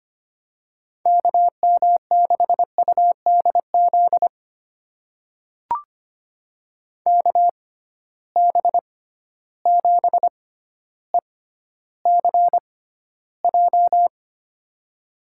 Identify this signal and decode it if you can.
Morse code, heard as a keyed tone, about 25 wpm, spaced as for 13 wpm, 700 Hz. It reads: KM6UDZ KB7ECJ
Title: USA Call Signs - 6 Characters - 6x Character Spacing - 25wpm